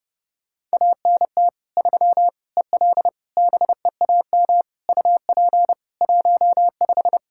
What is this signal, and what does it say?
Morse code, heard as a keyed tone, about 30 wpm, 700 Hz. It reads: ANT 3 EL BEAM UP 15
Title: K1USN SST QSO Practice 30wpm